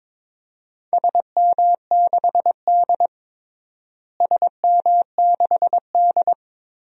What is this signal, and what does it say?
Morse code, heard as a keyed tone, about 22 wpm, 700 Hz. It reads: SM6D SM6D
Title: DXCC-100 Call Signs - 4 Characters - Repeated 2x - 22wpm